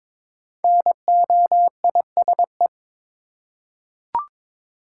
Morse code, heard as a keyed tone, about 22 wpm, 700 Hz. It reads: NOISE
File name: 5 Letter Words - First 3 Letters Given - 22wpm